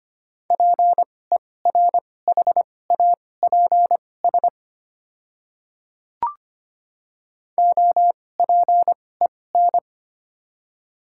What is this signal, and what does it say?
Morse code, heard as a keyed tone, about 25 wpm, spaced as for 18 wpm, 700 Hz. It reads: PERHAPS OPEN
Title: 2x Character Spacing - Top 500 Words - 25wpm